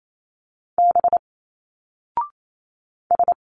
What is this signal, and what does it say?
Morse code, heard as a keyed tone, about 28 wpm, 700 Hz. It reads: B S
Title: Single Letters 28wpm